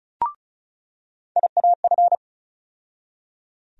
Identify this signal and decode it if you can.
Morse code, heard as a keyed tone, about 34 wpm, 700 Hz. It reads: IAF
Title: USA Call Sign Postfixes - 3 Letters 35wpm